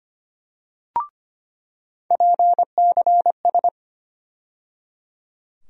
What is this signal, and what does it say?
Morse code, heard as a keyed tone, about 25 wpm, 700 Hz. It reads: PCS
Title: USA Call Sign Postfixes - 3 Letters 25wpm